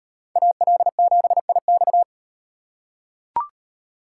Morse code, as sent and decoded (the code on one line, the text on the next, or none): .- .-.. --... .. -..-
AL7IX